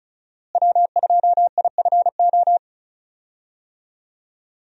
Morse code, heard as a keyed tone, about 34 wpm, 700 Hz. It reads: W2IFO
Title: USA Call Signs - 5 Characters - Repeated 2x - 35wpm